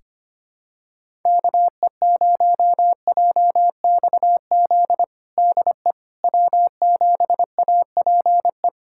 Morse code, heard as a keyed tone, about 25 wpm, 700 Hz. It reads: KE0JXZ DE W7APE
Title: USA Callsign DE Callsign 25wpm